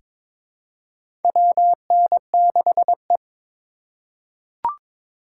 Morse code, heard as a keyed tone, about 21 wpm, 700 Hz. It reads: WN6E
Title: USA Call Signs - 4 Characters - 8x Character Spacing - 22wpm